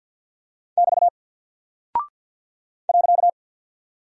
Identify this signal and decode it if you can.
Morse code, heard as a keyed tone, about 50 wpm, 700 Hz. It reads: <BT> .
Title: Single Character (Letter:Number:Punctuation:Prosign) 50wpm